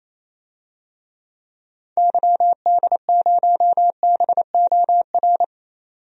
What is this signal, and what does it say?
Morse code, heard as a keyed tone, about 28 wpm, 700 Hz. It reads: YD0BOR
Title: DXCC-100 Call Signs - 6 Characters - 28wpm